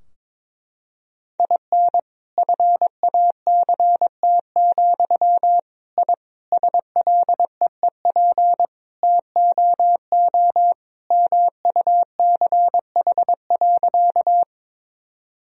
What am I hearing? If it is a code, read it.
Morse code, heard as a keyed tone, about 22 wpm, 700 Hz. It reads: IN FACT, I SLEEP TOO MUCH.
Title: Sentences from Top 1000 Words - Review Part 3 - 22wpm